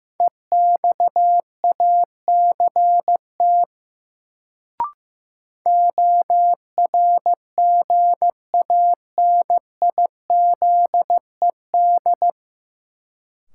Morse code, heard as a keyed tone, about 16 wpm, 700 Hz. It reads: EXACT ORGANIZED
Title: Sentences from Top 1300 Words - New Word Drill Alt - 15wpm